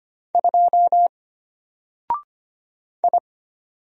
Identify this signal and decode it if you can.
Morse code, heard as a keyed tone, about 25 wpm, 700 Hz. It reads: 2 I
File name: Single-Letter-Number-25wpm